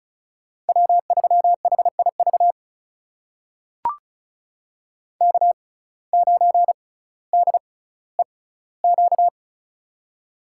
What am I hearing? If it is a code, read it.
Morse code, heard as a keyed tone, about 36 wpm, spaced as for 18 wpm, 700 Hz. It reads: W3HIV K9DEQ
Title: USA Call Signs - 5 Characters - 6x Character Spacing - 35wpm